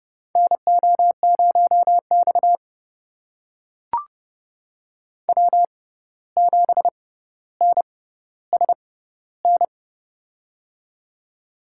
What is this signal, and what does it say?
Morse code, heard as a keyed tone, about 30 wpm, spaced as for 14 wpm, 700 Hz. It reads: NO0X W7NSN
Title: USA Call Signs - 6x Character Spacing - 30wpm